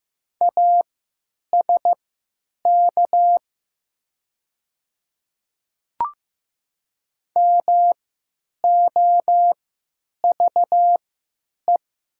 Morse code, heard as a keyed tone, about 15 wpm, spaced as for 9 wpm, 700 Hz. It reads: ASK MOVE
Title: 3x Character Spacing - Top 300 Words - 15wpm